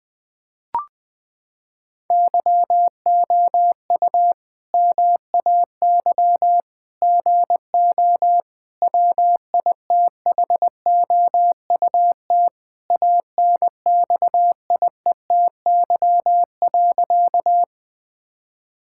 Morse code, heard as a keyed tone, about 20 wpm, 700 Hz. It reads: YOU MAY GO WITHOUT ANXIETY.
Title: Sentences from Top 1600 Words 20wpm